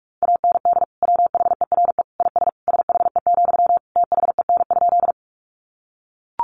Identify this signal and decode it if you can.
Morse code, heard as a keyed tone, about 44 wpm, 700 Hz. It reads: AND WHERE IS SHE, THEN?
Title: Sentences from Top 200 Words 45wpm